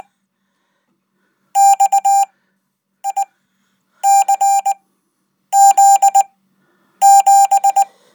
Morse code, heard as a keyed tone, about 20 wpm, spaced as for 8 wpm, 790 Hz. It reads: XICZ7
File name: Code